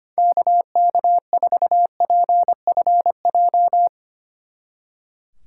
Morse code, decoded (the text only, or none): KK4PFJ